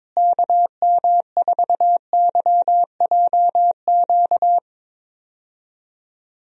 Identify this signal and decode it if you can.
Morse code, heard as a keyed tone, about 22 wpm, 700 Hz. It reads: KM4YJQ
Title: USA Call Signs - 6 Characters - Repeated 3x - 22wpm